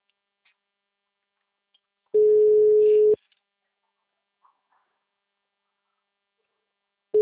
Ringback tone (Europe, 425 Hz).
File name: out- -2005- - - .84290